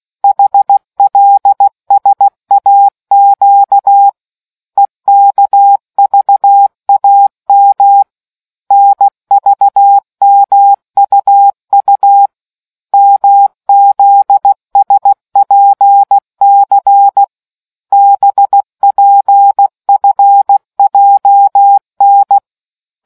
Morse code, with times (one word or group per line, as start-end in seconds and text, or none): 0.2-4.1 HLSAQ
4.8-8.0 EKVAM
8.7-12.3 NVMUU
12.9-17.2 MZSPC
17.9-22.4 BPFJN